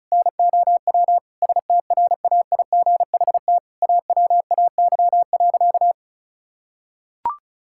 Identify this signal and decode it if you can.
Morse code, heard as a keyed tone, about 36 wpm, 700 Hz. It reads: NOW STRAIGHT AWAY.
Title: Sentences from Top 1000 Words - Review Part 6 - 35wpm